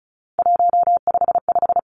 Morse code, heard as a keyed tone, about 36 wpm, 700 Hz. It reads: 155
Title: Three Challenging Letter-Number Combinations 35wpm